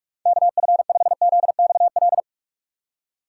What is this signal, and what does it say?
Morse code, heard as a keyed tone, about 42 wpm, 700 Hz. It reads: KF5ZXL
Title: USA Call Signs - 6 Characters - 6x Character Spacing - 45wpm